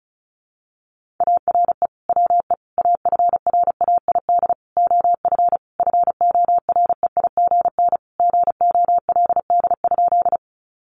Morse code, heard as a keyed tone, about 34 wpm, 700 Hz. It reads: ARE WE AFRAID OF FOREIGN GOLD?
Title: Sentences from Top 1500 Words 35wpm